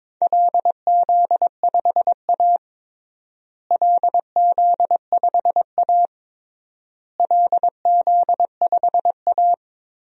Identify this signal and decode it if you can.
Morse code, heard as a keyed tone, about 21 wpm, 700 Hz. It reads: LZ5A LZ5A LZ5A
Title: DXCC-100 Call Signs - 4 Characters - Repeated 3x - 22wpm